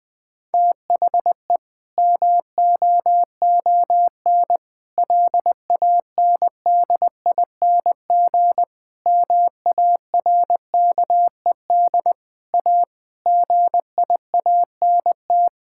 Morse code, heard as a keyed tone, about 20 wpm, 700 Hz. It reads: THE MOON LANDING MARKED A GIANT
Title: Difficult Sentences with Hints - 20wpm